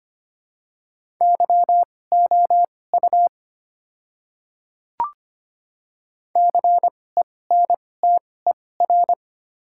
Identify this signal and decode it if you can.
Morse code, heard as a keyed tone, about 25 wpm, spaced as for 18 wpm, 690 Hz. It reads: YOU CENTER
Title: 2x Character Spacing - Top 500 Words - 25wpm